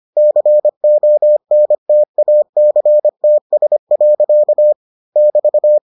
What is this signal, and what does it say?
Morse code, heard as a keyed tone, about 25 wpm, 600 Hz. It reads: CONTACTS. <BT>